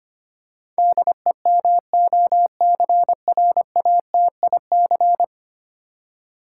Morse code, heard as a keyed tone, about 25 wpm, 700 Hz. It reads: DEMOCRATIC
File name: Ten Letter Words - 25wpm